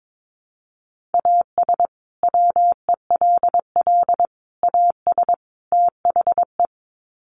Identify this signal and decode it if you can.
Morse code, heard as a keyed tone, about 22 wpm, 700 Hz. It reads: AS WELL AS THE
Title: Sets of 4 Words 22wpm